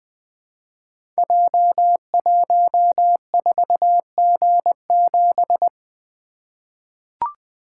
Morse code, heard as a keyed tone, about 20 wpm, 700 Hz. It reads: J14G7
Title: Code Groups - 5 Letter:Numbers - 20wpm